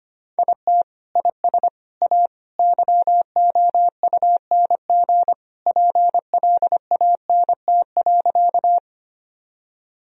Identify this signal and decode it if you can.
Morse code, heard as a keyed tone, about 25 wpm, 700 Hz. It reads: IT IS A YOUNG PLANT.
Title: Sentences from Top 1000 Words - Review Part 1 - 25wpm